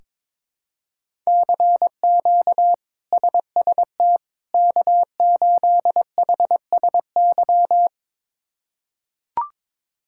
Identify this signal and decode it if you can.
Morse code, heard as a keyed tone, about 21 wpm, 700 Hz. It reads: CQ SST K8HSY
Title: K1USN SST QSO Practice - 22wpm